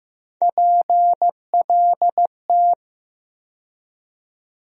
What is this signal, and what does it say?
Morse code, heard as a keyed tone, about 15 wpm, 700 Hz. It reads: PLT
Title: USA Call Sign Postfixes - 3 Letters 15wpm